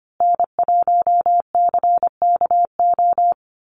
Morse code, heard as a keyed tone, about 25 wpm, 700 Hz. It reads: N1CKO
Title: USA Call Signs Repeated 3x 25wpm